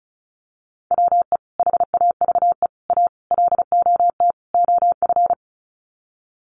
Morse code, heard as a keyed tone, about 34 wpm, 700 Hz. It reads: WE HAVE A LOT OF
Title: Sets of 5 Words 35wpm